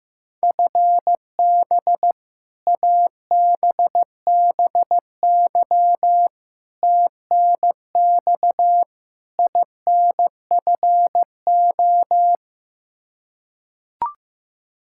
Morse code, heard as a keyed tone, about 15 wpm, 700 Hz. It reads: FB ABBY TNX INFO
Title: Practice QSO - Exchange 3 - More Info - 15wpm